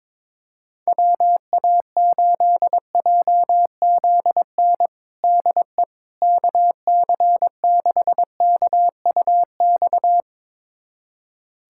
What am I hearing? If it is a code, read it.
Morse code, heard as a keyed tone, about 21 wpm, 700 Hz. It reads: WA8JZN DE KC6KUX
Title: USA Callsign DE Callsign - 6 Characters - 22wpm